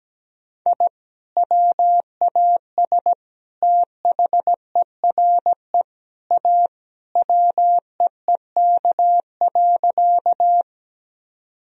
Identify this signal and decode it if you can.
Morse code, heard as a keyed tone, about 17 wpm, 700 Hz. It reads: I WAS THERE A WEEK.
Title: Sentences from Top 1000 Words - Review Part 2 - 17wpm